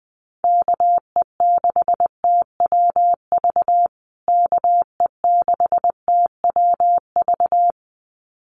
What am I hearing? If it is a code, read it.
Morse code, heard as a keyed tone, about 20 wpm, 700 Hz. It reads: KE6TWV KE6TWV
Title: USA Call Signs Repeated 2x 20wpm